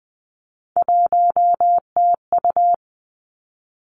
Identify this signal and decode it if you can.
Morse code, heard as a keyed tone, about 19 wpm, 700 Hz. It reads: 1TU
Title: Three Challenging Letter-Number Combinations 20wpm